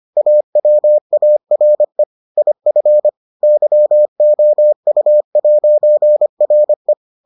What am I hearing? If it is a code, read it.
Morse code, heard as a keyed tone, about 25 wpm, 600 Hz. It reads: AWARE IF YOU'RE